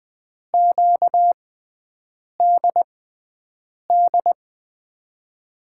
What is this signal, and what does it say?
Morse code, heard as a keyed tone, about 20 wpm, spaced as for 7 wpm, 700 Hz. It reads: QDD
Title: USA Call Signs - 6 Characters - 6x Character Spacing - 20wpm